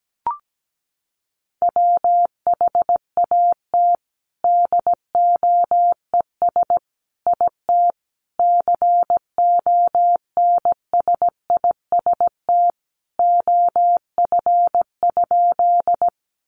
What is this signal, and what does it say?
Morse code, heard as a keyed tone, about 17 wpm, 700 Hz. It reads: WHAT DOES IT CONSIST OF?
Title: Sentences from Top 1400 Words 17wpm